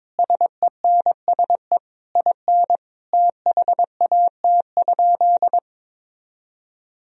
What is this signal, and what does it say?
Morse code, heard as a keyed tone, about 22 wpm, 700 Hz. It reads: SENSE IN THAT?
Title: Sentences from Top 1000 Words - Review Part 4 - 22wpm